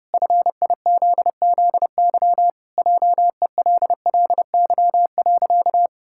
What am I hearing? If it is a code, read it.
Morse code, heard as a keyed tone, about 28 wpm, 700 Hz. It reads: FIZZY JELLY.
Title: Pangram Challenge v2 - 30wpm